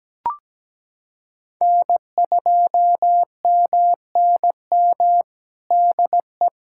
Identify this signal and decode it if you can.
Morse code, heard as a keyed tone, about 17 wpm, 700 Hz. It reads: N2MNM DE
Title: USA Callsign DE Callsign - 5 Characters - 17wpm